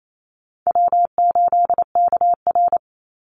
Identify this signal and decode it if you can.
Morse code, heard as a keyed tone, about 28 wpm, 700 Hz. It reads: W8KR
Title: Easy Call Signs - 28wpm